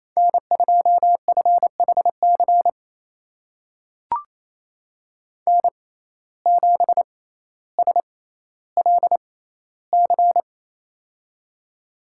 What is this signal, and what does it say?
Morse code, heard as a keyed tone, about 28 wpm, spaced as for 14 wpm, 700 Hz. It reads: N2FHC N7SLC
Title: USA Call Signs - 5 Characters - 6x Character Spacing - 28wpm